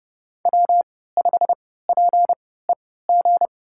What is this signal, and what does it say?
Morse code, heard as a keyed tone, about 31 wpm, spaced as for 17 wpm, 700 Hz. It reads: W5PEG
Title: USA Call Signs - 3x Character Spacing - 30wpm